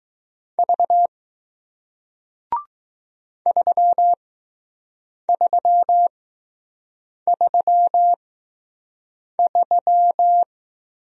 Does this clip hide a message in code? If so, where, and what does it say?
Yes: Morse code from 0.6 to 10.4 s, heard as a keyed tone, about 20 wpm, spaced as for 6 wpm, 700 Hz. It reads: V 3333